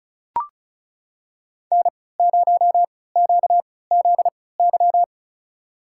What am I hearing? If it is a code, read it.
Morse code, heard as a keyed tone, about 36 wpm, spaced as for 20 wpm, 700 Hz. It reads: N0QZY
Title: USA Call Signs - 3x Character Spacing - 35wpm